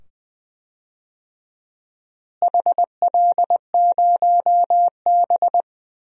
Morse code, heard as a keyed tone, about 20 wpm, 700 Hz. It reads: HL0B
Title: DXCC-100 Call Signs - 4 Characters - Repeated 2x - 20wpm